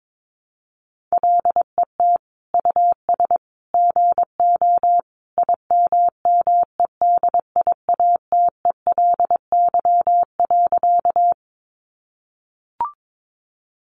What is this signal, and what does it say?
Morse code, heard as a keyed tone, about 22 wpm, 700 Hz. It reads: LET US GO IMMEDIATELY.
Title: Sentences from Top 1100 Words 22wpm